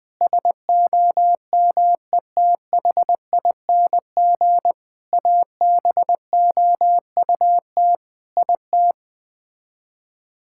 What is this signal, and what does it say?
Morse code, heard as a keyed tone, about 20 wpm, 700 Hz. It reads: SOMETHING ABOUT IT
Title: Build A Phrase - 5-Word Phrases - 20wpm